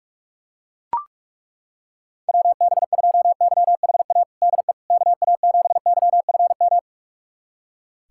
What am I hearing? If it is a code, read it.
Morse code, heard as a keyed tone, about 41 wpm, 700 Hz. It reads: WB2YHA DE KA7YFM
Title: USA Callsign DE Callsign - 6 Characters - 45wpm